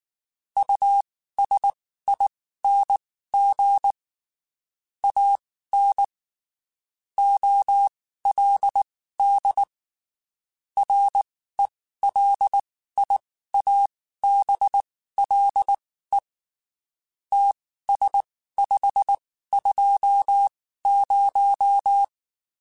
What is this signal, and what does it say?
Morse code, heard as a keyed tone, about 19 wpm, spaced as for 14 wpm, 800 Hz. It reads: USING AN OLD RELIABLE TS520
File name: (spaced) / QSOs Spaced-mp3s / WD0FBT 1 @18 sp